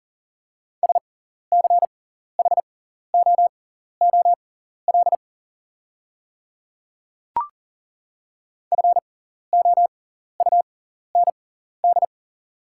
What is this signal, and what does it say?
Morse code, heard as a keyed tone, about 41 wpm, spaced as for 14 wpm, 690 Hz. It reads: SCHOOL FOUND
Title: 6x Character Spacing - Top 300 Words - 40wpm